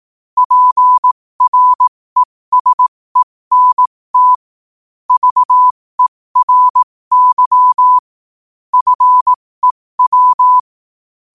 Morse code, heard as a keyed tone, about 18 wpm, spaced as for 15 wpm, 1000 Hz. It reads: PRESENT VERY FEW
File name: qotdfiles / 15wpm